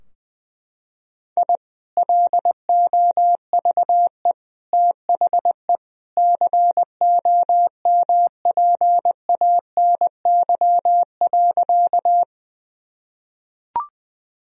Morse code, heard as a keyed tone, about 20 wpm, 700 Hz. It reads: I LOVE THE COMPANY.
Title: Sentences from Top 1000 Words - Review Part 3 - 20wpm